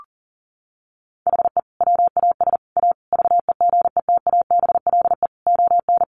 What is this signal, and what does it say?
Morse code, heard as a keyed tone, about 40 wpm, 700 Hz. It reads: HE WAS A VEGETABLE ON
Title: Sentences from Top 1300 Words 40wpm